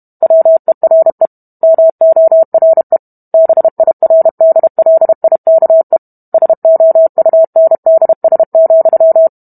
Morse code, heard as a keyed tone, about 32 wpm, 650 Hz. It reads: WERE MORE BIRDLIKE SOUNDS,